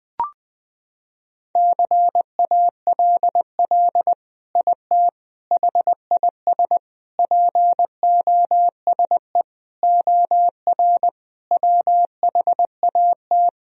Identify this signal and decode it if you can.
Morse code, heard as a keyed tone, about 20 wpm, 700 Hz. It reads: CALL IT HIS POSE OR WHAT
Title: Sentences from Top 1000 Words - Review Part 2 - 20wpm